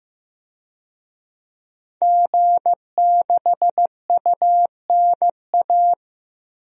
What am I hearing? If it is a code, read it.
Morse code, heard as a keyed tone, about 15 wpm, 700 Hz. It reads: G6UNA
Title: DXCC-100 Call Signs - 5 Characters - 15wpm